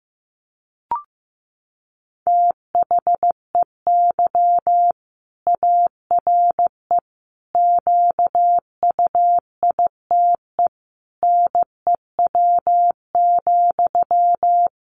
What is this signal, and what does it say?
Morse code, heard as a keyed tone, about 15 wpm, 700 Hz. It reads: THEY ARE QUITE NEW,
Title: Sentences from Top 700 Words 15wpm